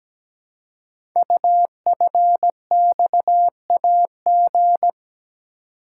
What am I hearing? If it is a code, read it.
Morse code, heard as a keyed tone, about 17 wpm, 700 Hz. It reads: UFXAG
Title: Code Groups - 5 Letters - 17wpm